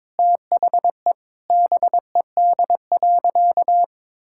Morse code, heard as a keyed tone, about 22 wpm, 700 Hz. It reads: THE BED.
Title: Sentences from Top 1000 Words - Review Part 6 - 22wpm